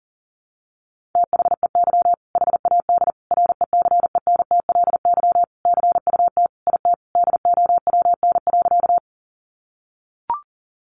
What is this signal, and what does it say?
Morse code, heard as a keyed tone, about 40 wpm, 700 Hz. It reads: THEY HAD RECENTLY CUT IT DOWN.
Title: Sentences from Top 1200 Words 40wpm